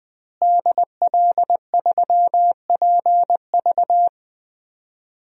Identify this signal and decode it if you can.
Morse code, heard as a keyed tone, about 20 wpm, 700 Hz. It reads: DL3PV
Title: DXCC-100 Call Signs - 5 Characters - Repeated 2x - 20wpm